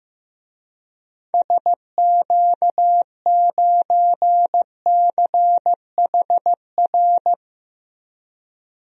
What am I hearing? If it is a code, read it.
Morse code, heard as a keyed tone, about 15 wpm, 700 Hz. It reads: SQ9CHR
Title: DXCC-100 Call Signs - 6 Characters - 15wpm